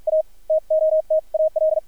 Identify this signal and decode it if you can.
Morse code, heard as a keyed tone, about 50 wpm, spaced as for 41 wpm, 630 Hz, with radio noise. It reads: A TOTAL
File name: archived / CRSnet